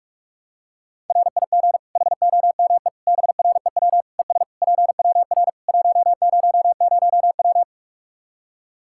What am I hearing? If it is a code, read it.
Morse code, heard as a keyed tone, about 46 wpm, 700 Hz. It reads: RIG HOME BREW ES PWR 100W
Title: Practice QSO - Exchange 2 - Info - 45wpm